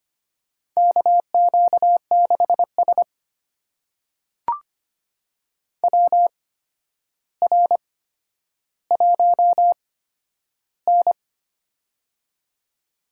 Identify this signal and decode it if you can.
Morse code, heard as a keyed tone, about 25 wpm, spaced as for 11 wpm, 700 Hz. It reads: KQ6S WR1N
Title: USA Call Signs - 4 Characters - 8x Character Spacing - 25wpm